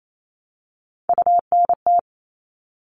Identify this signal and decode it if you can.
Morse code, heard as a keyed tone, about 28 wpm, 700 Hz. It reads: UNT
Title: All Cut Numbers - 3-Digit 28wpm